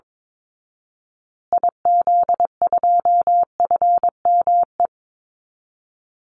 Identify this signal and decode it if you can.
Morse code, heard as a keyed tone, about 22 wpm, 700 Hz. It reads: IZ2FME
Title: Call Signs Repeated 3x - 22wpm